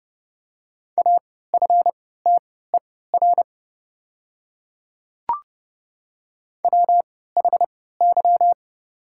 Morse code, heard as a keyed tone, about 30 wpm, spaced as for 17 wpm, 700 Hz. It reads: AFTER WHY
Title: 3x Character Spacing - Top 200 Words - 30wpm